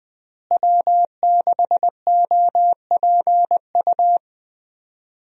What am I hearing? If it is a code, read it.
Morse code, heard as a keyed tone, about 21 wpm, 700 Hz. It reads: W6OPU
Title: USA Call Signs - 5 Characters - Repeated 3x - 20wpm